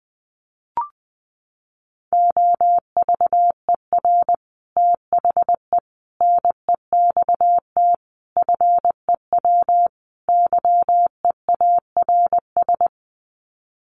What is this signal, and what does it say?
Morse code, heard as a keyed tone, about 20 wpm, 700 Hz. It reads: OVER THE NEXT FEW YEARS
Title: Sets of 5 Words Encore 20wpm